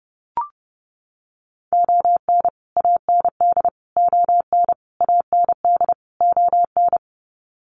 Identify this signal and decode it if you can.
Morse code, heard as a keyed tone, about 30 wpm, 700 Hz. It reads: ON AND ON AND ON
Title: Sets of 5 Words Encore2 30wpm